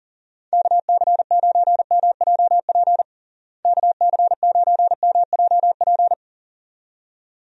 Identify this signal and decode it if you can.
Morse code, heard as a keyed tone, about 40 wpm, 700 Hz. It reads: KC9MJP KC9MJP
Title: USA Call Signs - 6 Characters - Repeated 2x - 40wpm